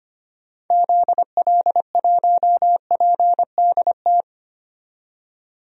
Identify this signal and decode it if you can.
Morse code, heard as a keyed tone, about 25 wpm, 700 Hz. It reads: ZL1PDT
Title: DXCC-100 Call Signs - 6 Characters - 25wpm